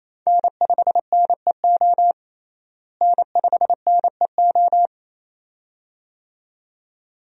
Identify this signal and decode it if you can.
Morse code, heard as a keyed tone, about 28 wpm, 700 Hz. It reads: N5NEO N5NEO
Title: USA Call Signs - 5 Characters - Repeated 2x - 28wpm